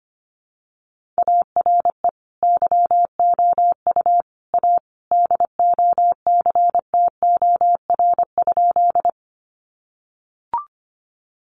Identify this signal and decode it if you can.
Morse code, heard as a keyed tone, about 25 wpm, 700 Hz. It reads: ARE YOU A DOCTOR?
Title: Sentences from Top 1500 Words 25wpm